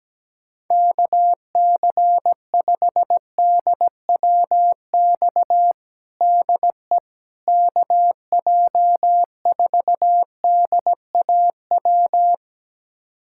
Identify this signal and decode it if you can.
Morse code, heard as a keyed tone, about 17 wpm, 700 Hz. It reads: KC5DWX DE KJ4DAW
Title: USA Callsign DE Callsign - 6 Characters - 17wpm